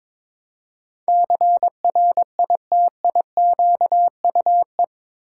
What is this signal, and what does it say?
Morse code, heard as a keyed tone, about 21 wpm, 700 Hz. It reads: CRITIQUE
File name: practice / 22wpm / Eight Letter Words - 22wpm